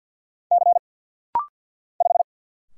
Morse code, heard as a keyed tone, about 47 wpm, 690 Hz. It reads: C 5
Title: Single Letter-Number Alt - Rapid-Fire 50wpm